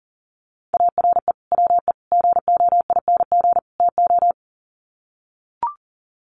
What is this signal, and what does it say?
Morse code, heard as a keyed tone, about 40 wpm, 700 Hz. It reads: ARE WE GOING TO
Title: Sets of 4 Words 40wpm